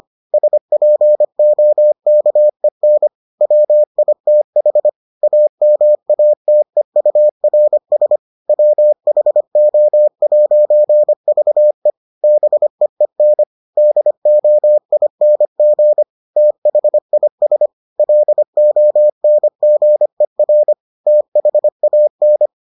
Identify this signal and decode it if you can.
Morse code, heard as a keyed tone, about 25 wpm, 600 Hz. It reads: SPOKEN WITH AMATEURS WHO'VE BEEN DOING THIS LONGER THAN